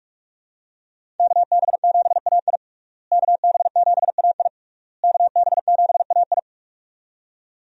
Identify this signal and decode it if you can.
Morse code, heard as a keyed tone, about 46 wpm, 700 Hz. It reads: KB7AI KB7AI KB7AI
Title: USA Call Signs - 5 Characters - Repeated 3x - 45wpm